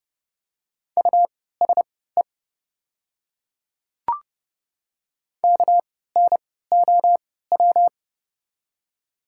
Morse code, heard as a keyed tone, about 30 wpm, spaced as for 17 wpm, 700 Hz. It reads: USE KNOW
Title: 3x Character Spacing - Top 200 Words - 30wpm